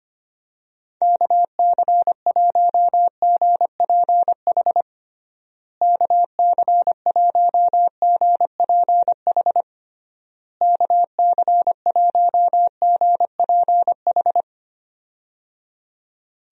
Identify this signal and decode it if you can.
Morse code, heard as a keyed tone, about 25 wpm, 700 Hz. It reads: KC1GPH KC1GPH KC1GPH
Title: USA Call Signs - 6 Characters - Repeated 3x - 25wpm